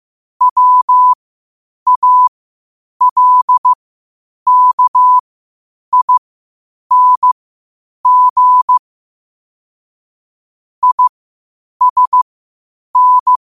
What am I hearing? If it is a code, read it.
Morse code, heard as a keyed tone, about 15 wpm, spaced as for 8 wpm, 1000 Hz. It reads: WALKING ISN
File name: qotdfiles / 8wpm